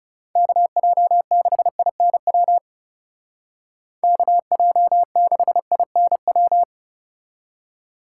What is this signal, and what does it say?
Morse code, heard as a keyed tone, about 37 wpm, 700 Hz. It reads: KJ6INW KJ6INW
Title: Speed Racing - USA 2x3 Call Signs 30wpm